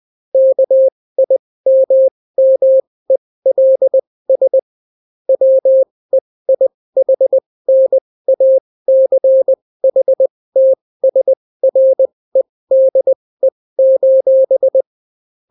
Morse code, heard as a keyed tone, about 20 wpm, spaced as for 16 wpm, 530 Hz. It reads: KIMMELS WEIHNACHTSREDE: